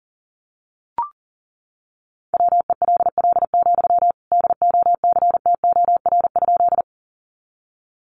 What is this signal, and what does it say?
Morse code, heard as a keyed tone, about 40 wpm, 700 Hz. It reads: WELL, DOCTOR?
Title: Sentences from Top 1500 Words 40wpm